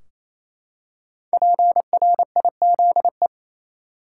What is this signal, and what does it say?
Morse code, heard as a keyed tone, about 28 wpm, 700 Hz. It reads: PRIZE
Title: Five Letter Words - 28wpm